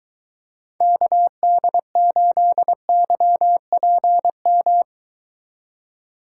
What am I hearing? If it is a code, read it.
Morse code, heard as a keyed tone, about 23 wpm, 700 Hz. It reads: KD8YPM